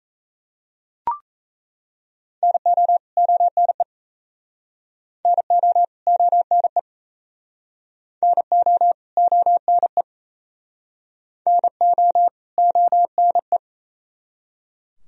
Morse code, heard as a keyed tone, about 37 wpm, 700 Hz. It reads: NO ONE NO ONE NO ONE NO ONE